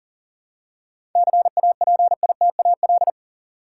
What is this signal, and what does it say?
Morse code, heard as a keyed tone, about 40 wpm, 700 Hz. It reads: CAPITAL
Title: Seven Letter Words - 40wpm